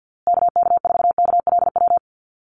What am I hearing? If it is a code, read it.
Morse code, heard as a keyed tone, about 50 wpm, 700 Hz. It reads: KK4KLW
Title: USA Call Signs Repeated 3x 50wpm